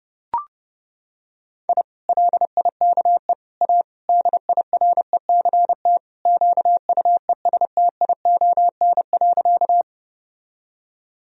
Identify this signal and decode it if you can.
Morse code, heard as a keyed tone, about 32 wpm, 700 Hz. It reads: I LIKE A DIRECT QUESTION.
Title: Sentences from Top 1000 Words - Review Part 2 - 30wpm